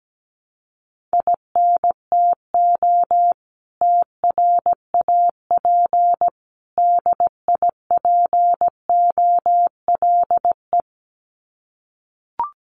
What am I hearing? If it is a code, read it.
Morse code, heard as a keyed tone, about 17 wpm, 700 Hz. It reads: INTO TRAP DIPOLE
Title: Common QSO Phrases 17wpm